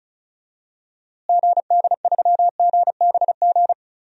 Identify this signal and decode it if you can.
Morse code, heard as a keyed tone, about 36 wpm, 700 Hz. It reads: GD3GBG
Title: DXCC-100 Call Signs - 6 Characters - Repeated 3x - 35wpm